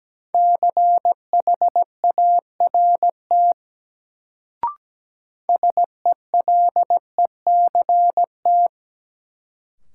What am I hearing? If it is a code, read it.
Morse code, heard as a keyed tone, about 17 wpm, 700 Hz. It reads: CHART SELECT
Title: Sentences from Top 1300 Words - New Word Drill - 17wpm